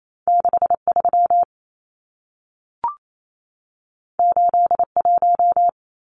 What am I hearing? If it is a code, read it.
Morse code, heard as a keyed tone, about 28 wpm, 700 Hz. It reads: 63 81
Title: Numbers - 2-Digit 28wpm